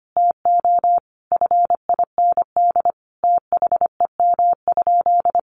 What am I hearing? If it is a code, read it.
Morse code, heard as a keyed tone, about 25 wpm, 700 Hz. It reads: TO FIND THEM?
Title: Sentences from Top 1100 Words 25wpm